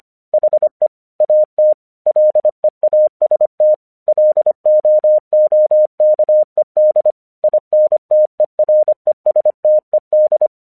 Morse code, heard as a keyed tone, about 25 wpm, 610 Hz. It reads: HE AT LEAST LOOKED INTERESTED
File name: 1 English 25wpm / AQC23